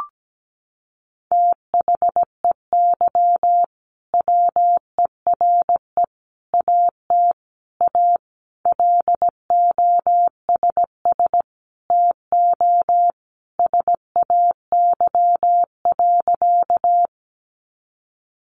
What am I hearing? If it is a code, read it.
Morse code, heard as a keyed tone, about 17 wpm, 700 Hz. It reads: THEY WERE AT A LOSS TO SAY.
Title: Sentences from Top 1400 Words 17wpm